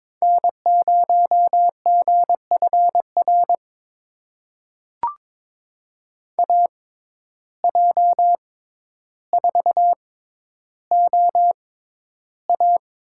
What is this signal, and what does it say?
Morse code, heard as a keyed tone, about 22 wpm, spaced as for 11 wpm, 700 Hz. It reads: N0GFR AJ4OA
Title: USA Call Signs - 5 Characters - 6x Character Spacing - 22wpm